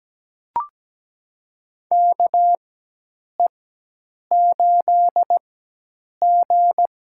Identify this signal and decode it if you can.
Morse code, heard as a keyed tone, about 17 wpm, spaced as for 8 wpm, 700 Hz. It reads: KE8G
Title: USA Call Signs - 4 Characters - 4x Character Spacing - 17wpm